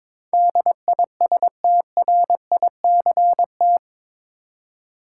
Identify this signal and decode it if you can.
Morse code, heard as a keyed tone, about 22 wpm, 700 Hz. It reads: DISTRICT
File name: practice / 22wpm / Eight Letter Words - 22wpm